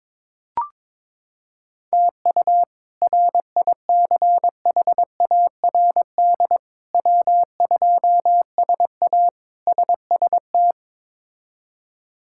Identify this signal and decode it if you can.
Morse code, heard as a keyed tone, about 22 wpm, 700 Hz. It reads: TU RICHARD W2SA SST
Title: K1USN SST QSO Practice - 22wpm